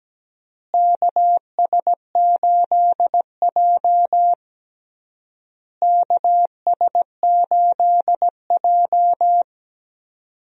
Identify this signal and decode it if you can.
Morse code, heard as a keyed tone, about 17 wpm, 700 Hz. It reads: KS8J KS8J
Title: USA Call Signs - 4 Characters - Repeated 3x - 17wpm